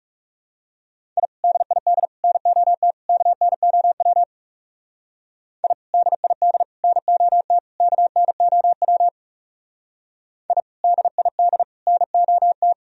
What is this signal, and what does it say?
Morse code, heard as a keyed tone, about 37 wpm, 700 Hz. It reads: I DID NOT KNOW I DID NOT KNOW I DID NOT
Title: Speed-Racing - Sets of 4 Words 30wpm